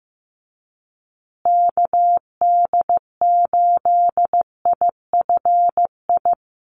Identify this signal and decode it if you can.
Morse code, heard as a keyed tone, about 15 wpm, 700 Hz. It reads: KD8IFI
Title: USA Call Signs Repeated 2x 15wpm